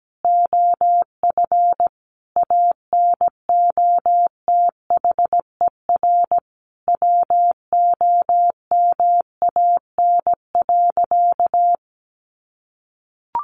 Morse code, heard as a keyed tone, about 17 wpm, 700 Hz. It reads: OF ANOTHER WOMAN.
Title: Sentences from Top 1400 Words 17wpm